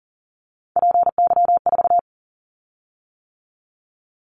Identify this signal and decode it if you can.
Morse code, heard as a keyed tone, about 42 wpm, 700 Hz. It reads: PY4
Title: Callsign Prefixes for Top 100 DXCC - 40wpm